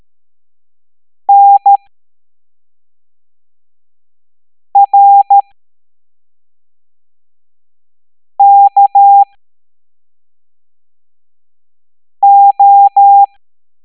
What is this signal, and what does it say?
Morse code, heard as a keyed tone, about 13 wpm, spaced as for 3 wpm, 800 Hz. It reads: NRKO